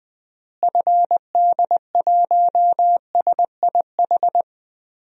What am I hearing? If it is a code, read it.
Morse code, heard as a keyed tone, about 21 wpm, 700 Hz. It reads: FD1SIH
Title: DXCC-100 Call Signs - 6 Characters - Repeated 3x - 20wpm